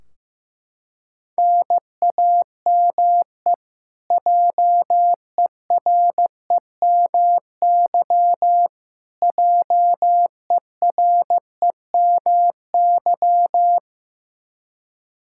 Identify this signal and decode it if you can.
Morse code, heard as a keyed tone, about 15 wpm, 700 Hz. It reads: NAME JEREMY JEREMY
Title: Practice QSO - Exchange 1 - Beginning - 15wpm